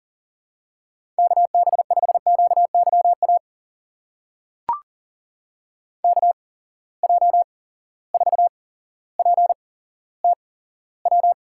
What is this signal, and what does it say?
Morse code, heard as a keyed tone, about 42 wpm, spaced as for 17 wpm, 700 Hz. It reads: KB5QYA KJ4PTW
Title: USA Call Signs - 6 Characters - 8x Character Spacing - 40wpm